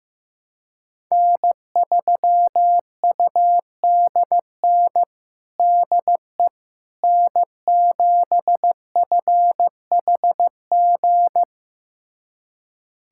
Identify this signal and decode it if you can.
Morse code, heard as a keyed tone, about 15 wpm, 700 Hz. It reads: N3UDN DE N7FHG
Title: USA Callsign DE Callsign - 5 Characters - 15wpm